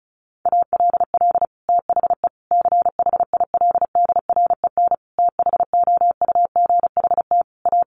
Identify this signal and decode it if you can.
Morse code, heard as a keyed tone, about 34 wpm, 700 Hz. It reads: ALL THE CHILDREN THOUGHT A
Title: Sentences from Top 800 Words 35wpm